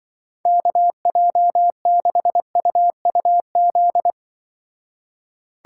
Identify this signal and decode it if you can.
Morse code, heard as a keyed tone, about 24 wpm, 700 Hz. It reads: KJ6UUZ
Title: Speed Racing - USA Call Signs 20wpm